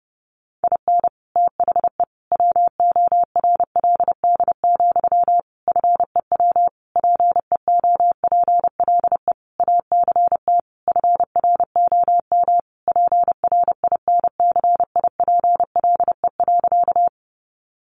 Morse code, heard as a keyed tone, about 30 wpm, 700 Hz. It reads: IN THE WORLD, FEW PEOPLE ACT FROM PRINCIPLE.
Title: Sentences from Top 1400 Words 30wpm